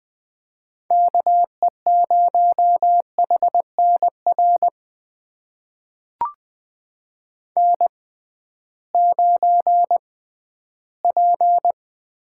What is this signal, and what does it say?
Morse code, heard as a keyed tone, about 20 wpm, spaced as for 13 wpm, 700 Hz. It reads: KE0HNR N9P
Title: USA Call Signs - 6x Character Spacing - 20wpm